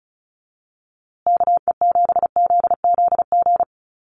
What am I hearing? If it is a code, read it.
Morse code, heard as a keyed tone, about 34 wpm, 700 Hz. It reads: KE7ZZG
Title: USA Call Signs Repeated 2x 35wpm